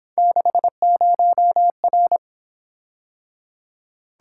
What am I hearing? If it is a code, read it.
Morse code, heard as a keyed tone, about 26 wpm, 700 Hz. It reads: 60R
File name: Speed-Racing - Three Letter-Number Combinations - 22wpm